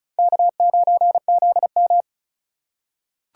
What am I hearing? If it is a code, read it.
Morse code, heard as a keyed tone, about 36 wpm, 690 Hz. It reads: K9ZM
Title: USA Call Signs - 4 Characters - 2x Character Spacing - 35wpm